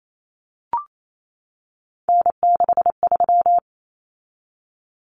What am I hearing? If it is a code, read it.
Morse code, heard as a keyed tone, about 28 wpm, 700 Hz. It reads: N63
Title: Common Cut Numbers - 3-Digit 28wpm